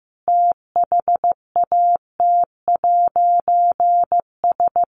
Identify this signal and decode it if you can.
Morse code, heard as a keyed tone, about 15 wpm, 700 Hz. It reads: THAT'S
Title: A Sound Of Thunder 15wpm